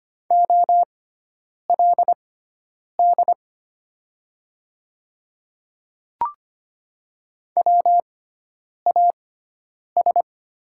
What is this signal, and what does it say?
Morse code, heard as a keyed tone, about 25 wpm, spaced as for 9 wpm, 700 Hz. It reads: OLD WAS
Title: 6x Character Spacing - Top 300 Words - 25wpm